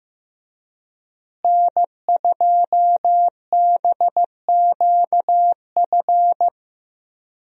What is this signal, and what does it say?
Morse code, heard as a keyed tone, about 15 wpm, 700 Hz. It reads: N2BQF
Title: USA Call Signs - 5 Characters - Repeated 3x - 15wpm